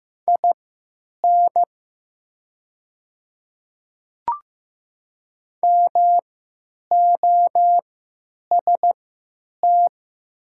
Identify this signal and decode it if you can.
Morse code, heard as a keyed tone, about 15 wpm, spaced as for 9 wpm, 700 Hz. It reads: IN MOST
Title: 3x Character Spacing - Top 100 Words - 15wpm